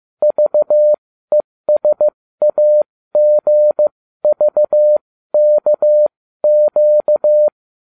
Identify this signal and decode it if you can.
Morse code, heard as a keyed tone, about 15 wpm, spaced as for 13 wpm, 600 Hz. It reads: VESAGVKQ